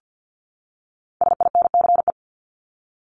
Morse code, heard as a keyed tone, about 48 wpm, 700 Hz. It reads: SINCE
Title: Top 300 Words 50wpm